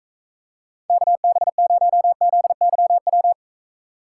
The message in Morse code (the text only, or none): KB0ZYW